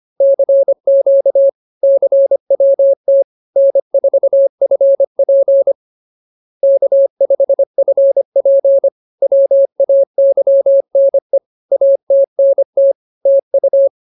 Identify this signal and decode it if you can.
Morse code, heard as a keyed tone, about 25 wpm, 550 Hz. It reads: CQ CWT N4FP K5FP WAYNE ATNT TU